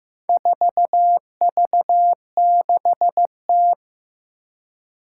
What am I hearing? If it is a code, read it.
Morse code, heard as a keyed tone, about 15 wpm, 700 Hz. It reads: 4V6T
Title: Four Challenging Letter-Number Combinations 15wpm